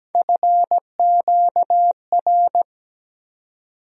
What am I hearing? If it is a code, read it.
Morse code, heard as a keyed tone, about 17 wpm, 700 Hz. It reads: FQR
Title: USA Call Sign Postfixes - 3 Letters 17wpm